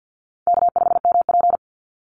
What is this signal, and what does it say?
Morse code, heard as a keyed tone, about 50 wpm, 700 Hz. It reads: K5MP
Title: Easy Call Signs Repeated 2x - 50wpm